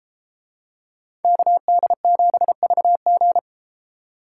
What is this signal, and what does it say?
Morse code, heard as a keyed tone, about 32 wpm, 700 Hz. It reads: KD7VG